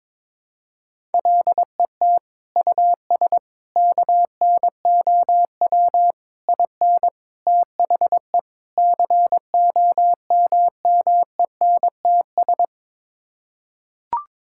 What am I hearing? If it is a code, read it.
Morse code, heard as a keyed tone, about 22 wpm, 700 Hz. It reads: LET US KNOW IN THE COMMENTS
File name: practice / 22wpm / Sets of 6 Words - 22wpm